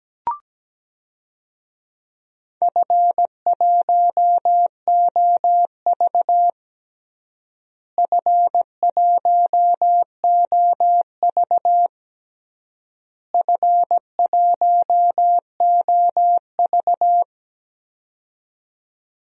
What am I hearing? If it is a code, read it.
Morse code, heard as a keyed tone, about 17 wpm, 700 Hz. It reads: F1OV F1OV F1OV
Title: DXCC-100 Call Signs - 4 Characters - Repeated 3x - 17wpm